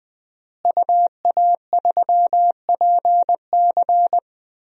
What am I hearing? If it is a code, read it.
Morse code, heard as a keyed tone, about 20 wpm, 700 Hz. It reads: UA3PC
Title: DXCC-100 Call Signs - 5 Characters - Repeated 3x - 20wpm